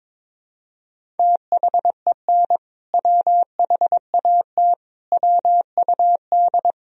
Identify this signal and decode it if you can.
Morse code, heard as a keyed tone, about 21 wpm, 700 Hz. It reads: THEN WHAT WUD
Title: Sentences from Top 400 Words and Abbreviations 22wpm